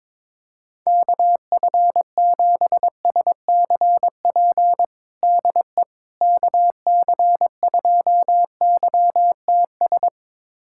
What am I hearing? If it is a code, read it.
Morse code, heard as a keyed tone, about 21 wpm, 700 Hz. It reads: KF7SCP DE KC2YTS
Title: USA Callsign DE Callsign - 6 Characters - 22wpm